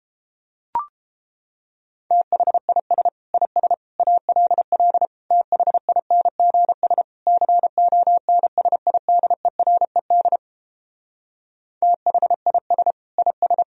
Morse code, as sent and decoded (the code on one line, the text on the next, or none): - .... .. ... / .. ... / .- .-.. .-.. / - .... .. -. --. ... / -.-. --- -. ... .. -.. .-. -.. / - .... .. ... / .. ...
THIS IS ALL THINGS CONSIDERED THIS IS